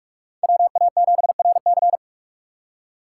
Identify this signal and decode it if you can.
Morse code, heard as a keyed tone, about 45 wpm, 700 Hz. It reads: WA7RC
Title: USA Call Signs - 5 Characters - 4x Character Spacing - 45wpm